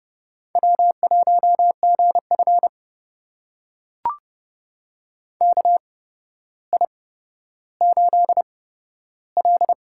Morse code, heard as a keyed tone, about 30 wpm, spaced as for 13 wpm, 700 Hz. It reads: W1GF KI8L
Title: USA Call Signs - 4 Characters - 8x Character Spacing - 30wpm